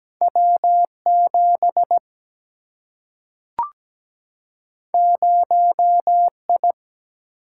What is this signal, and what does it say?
Morse code, heard as a keyed tone, about 17 wpm, 700 Hz. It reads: W 7 0 I
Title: Single-Letter Single-Number 17wpm